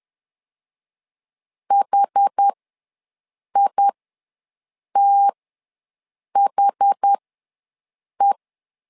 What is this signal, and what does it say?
Morse code, heard as a keyed tone, about 10 wpm, spaced as for 6 wpm, 800 Hz. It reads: HITHE
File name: LEZIONI / 2 UHDN / 2 UH ws 10wpm 2029